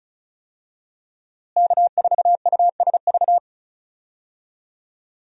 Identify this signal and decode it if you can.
Morse code, heard as a keyed tone, about 36 wpm, 700 Hz. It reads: K4USV
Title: USA Call Signs - 5 Characters - Repeated 2x - 35wpm